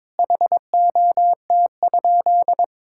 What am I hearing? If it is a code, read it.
Morse code, heard as a keyed tone, about 22 wpm, 700 Hz. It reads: HOT?
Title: Sentences from Top 1000 Words - Review Part 6 - 22wpm